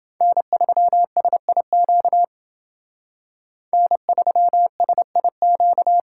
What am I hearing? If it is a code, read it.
Morse code, heard as a keyed tone, about 28 wpm, 700 Hz. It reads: N3SIQ N3SIQ